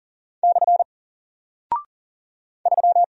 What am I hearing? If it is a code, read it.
Morse code, heard as a keyed tone, about 40 wpm, 700 Hz. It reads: / 3
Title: Single Character (Letter:Number:Punctuation:Prosign) Alt 40wpm